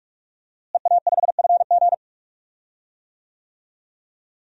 Morse code, heard as a keyed tone, about 46 wpm, 700 Hz. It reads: EA5FG
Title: DXCC-100 Call Signs - 5 Characters - Repeated 2x - 45wpm